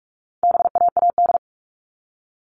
Morse code, heard as a keyed tone, about 44 wpm, 700 Hz. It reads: BAAD
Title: All Cut Numbers - 4-Digit 45wpm